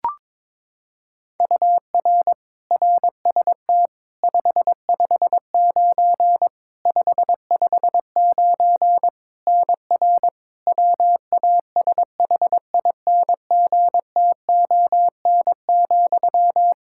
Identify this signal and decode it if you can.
Morse code, heard as a keyed tone, about 21 wpm, 700 Hz. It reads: UR RST 559 559 NR WASHINGTON,
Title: Practice QSO - Exchange 1 - Beginning - 22wpm